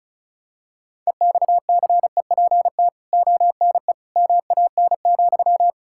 Morse code, heard as a keyed tone, about 34 wpm, 700 Hz. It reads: EXCEPT ONE MAN,
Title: Sentences from Top 1600 Words 35wpm